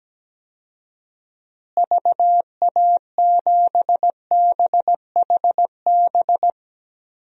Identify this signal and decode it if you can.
Morse code, heard as a keyed tone, about 17 wpm, 700 Hz. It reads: VA7BHB
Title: DXCC-100 Call Signs - 6 Characters - Repeated 3x - 17wpm